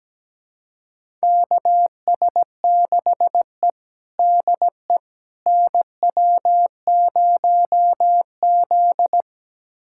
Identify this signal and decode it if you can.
Morse code, heard as a keyed tone, about 17 wpm, 700 Hz. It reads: KS6E DE NW0Z
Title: USA Callsign DE Callsign - 4 Characters - 17wpm